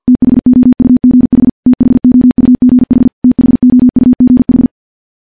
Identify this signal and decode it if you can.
Morse code, heard as a keyed tone, about 49 wpm, 250 Hz. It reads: THOUGH THOUGH THOUGH
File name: morse / morse-training-50WPM